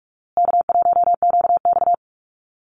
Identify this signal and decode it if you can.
Morse code, heard as a keyed tone, about 44 wpm, 700 Hz. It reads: K1QX
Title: Call Signs Repeated 2x - 45wpm